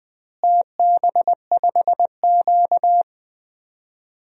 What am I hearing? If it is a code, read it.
Morse code, heard as a keyed tone, about 19 wpm, 700 Hz. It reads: TB5Q
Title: DXCC-100 Call Signs - 4 Characters - Repeated 3x - 20wpm